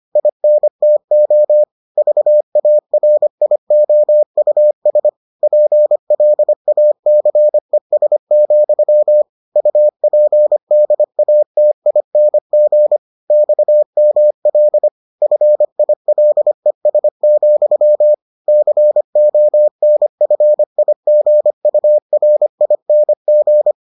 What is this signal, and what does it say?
Morse code, heard as a keyed tone, about 25 wpm, 600 Hz. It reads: INTO VARIOUS PLACES, UPDATING XML FILES, CONFIGURING